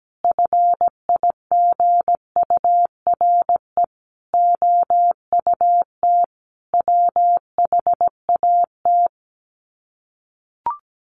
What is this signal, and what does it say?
Morse code, heard as a keyed tone, about 17 wpm, 700 Hz. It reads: FIGURE OUT WHAT
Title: Sets of 5 Words 17wpm